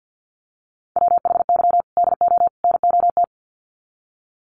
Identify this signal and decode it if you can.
Morse code, heard as a keyed tone, about 50 wpm, 700 Hz. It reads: WHY DO NOT
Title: Sets of 3 Words 50wpm